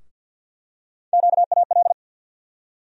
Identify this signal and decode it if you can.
Morse code, heard as a keyed tone, about 52 wpm, 700 Hz. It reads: QAL
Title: USA Call Sign Postfixes - 3 Letters 50wpm